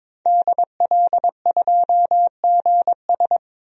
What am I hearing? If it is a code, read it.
Morse code, heard as a keyed tone, about 22 wpm, 700 Hz. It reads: DL2GS
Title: DXCC-100 Call Signs - 5 Characters - Repeated 3x - 22wpm